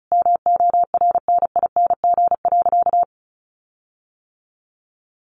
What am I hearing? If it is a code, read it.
Morse code, heard as a keyed tone, about 34 wpm, 700 Hz. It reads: MORNING.
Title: Sentences from Top 1500 Words 35wpm